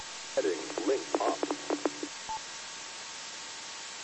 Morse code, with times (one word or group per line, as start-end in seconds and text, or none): none